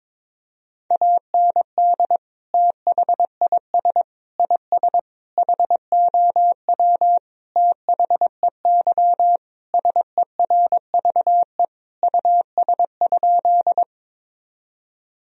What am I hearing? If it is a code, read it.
Morse code, heard as a keyed tone, about 21 wpm, 700 Hz. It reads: AND THIS IS HOW THEY SERVE US?
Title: Sentences from Top 1000 Words - Review Part 2 - 22wpm